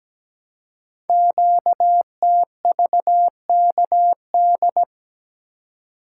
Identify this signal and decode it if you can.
Morse code, heard as a keyed tone, about 17 wpm, 700 Hz. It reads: QTVKD
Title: Code Groups - 5 Letter:Numbers - 17wpm